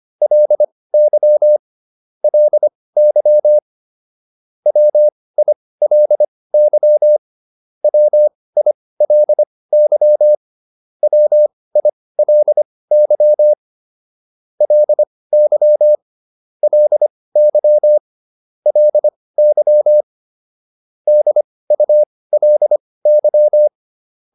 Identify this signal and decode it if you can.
Morse code, heard as a keyed tone, about 25 wpm, spaced as for 18 wpm, 600 Hz. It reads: LY LY WILY WILY WILY LY LY LY DULY